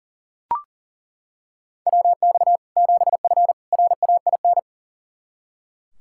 Morse code, heard as a keyed tone, about 40 wpm, 700 Hz. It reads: WX 7F RAIN
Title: Practice QSO - Exchange 2 - Info - 40wpm